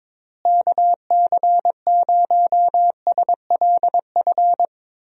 Morse code, heard as a keyed tone, about 22 wpm, 700 Hz. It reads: KC0SLF
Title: USA Call Signs - 6 Characters - Repeated 3x - 22wpm